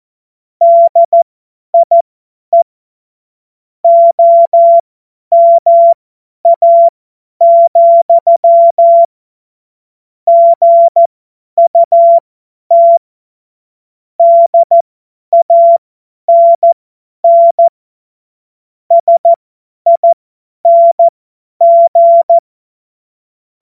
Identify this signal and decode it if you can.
Morse code, heard as a keyed tone, about 14 wpm, spaced as for 10 wpm, 680 Hz. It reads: DIE OMA, GUT DANN SING
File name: Morsekurs / Witze / Witz 12. Oktober 2021, 50 F70